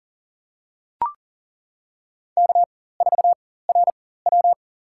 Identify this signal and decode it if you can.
Morse code, heard as a keyed tone, about 37 wpm, spaced as for 18 wpm, 700 Hz. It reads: K4RW
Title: USA Call Signs - 4x Character Spacing - 40wpm